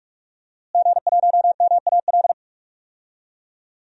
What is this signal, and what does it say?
Morse code, heard as a keyed tone, about 47 wpm, 700 Hz. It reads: G1MAL
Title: DXCC-100 Call Signs - 5 Characters - Repeated 3x - 45wpm